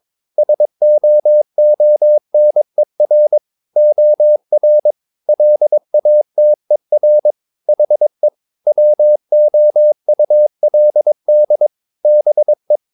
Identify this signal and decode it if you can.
Morse code, heard as a keyed tone, about 22 wpm, 600 Hz. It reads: SOONER OR LATER HE WOULD BE
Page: Romanen Never af Ken Follet på engelsk.